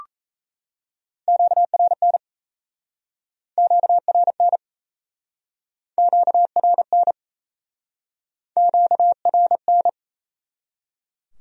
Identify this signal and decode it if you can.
Morse code, heard as a keyed tone, about 37 wpm, 700 Hz. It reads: QRN QRN QRN QRN